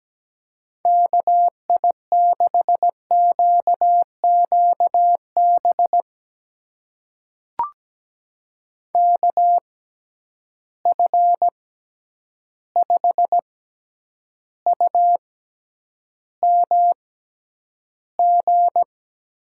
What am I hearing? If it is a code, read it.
Morse code, heard as a keyed tone, about 17 wpm, spaced as for 9 wpm, 700 Hz. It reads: KI6QQB KF5UMG
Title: USA Call Signs - 6 Characters - 6x Character Spacing - 17wpm